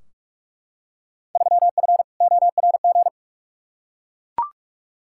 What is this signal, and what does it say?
Morse code, heard as a keyed tone, about 42 wpm, 700 Hz. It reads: 3F ORG
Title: Field Day Exchanges - 45wpm